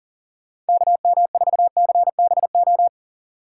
Morse code, heard as a keyed tone, about 51 wpm, 700 Hz. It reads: KM4CBO